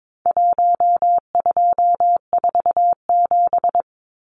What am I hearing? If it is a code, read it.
Morse code, heard as a keyed tone, about 22 wpm, 700 Hz. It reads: 1247
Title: Numbers - 4-Digit 22wpm